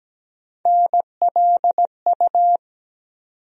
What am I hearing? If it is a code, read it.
Morse code, heard as a keyed tone, about 17 wpm, 700 Hz. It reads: NLU
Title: USA Call Sign Postfixes - 3 Letters 17wpm